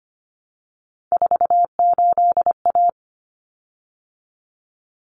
Morse code, heard as a keyed tone, about 25 wpm, 700 Hz. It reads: 48A
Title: Common Cut Numbers - 3-Digit 25wpm